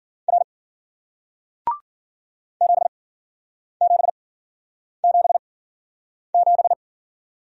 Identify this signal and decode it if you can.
Morse code, heard as a keyed tone, about 49 wpm, spaced as for 9 wpm, 700 Hz. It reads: R 7777
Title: Speed-Racing Single Letter-Number 40wpm